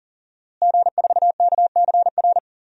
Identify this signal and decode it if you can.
Morse code, heard as a keyed tone, about 40 wpm, 690 Hz. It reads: G4KCR
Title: DXCC-100 Call Signs - 5 Characters - Repeated 2x - 40wpm